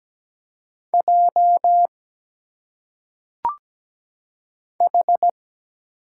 Morse code, heard as a keyed tone, about 17 wpm, 700 Hz. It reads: J H
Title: Single Letters - Phonetic Pronunciation 17wpm